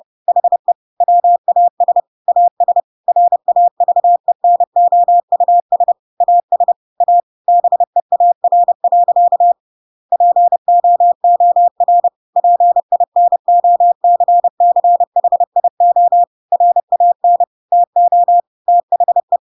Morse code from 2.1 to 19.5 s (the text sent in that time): AS RAVENOUS AS A BEAR. POOR PINOCCHIO RAN TO THE